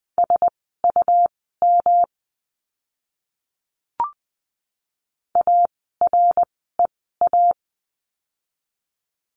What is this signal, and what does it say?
Morse code, heard as a keyed tone, about 20 wpm, spaced as for 14 wpm, 700 Hz. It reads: SUM AREA
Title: 108 Lesson 18f - Letter m - Words 20wpm